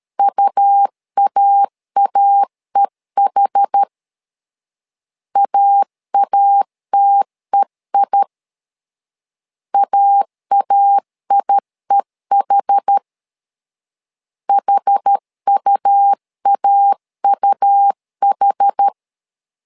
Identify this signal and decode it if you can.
Morse code, heard as a keyed tone, about 12 wpm, 800 Hz. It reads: UAAEH AATEI AAIEH HUAUH
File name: LEZIONI / 2 UHDN / 2 UH 12wpm 1439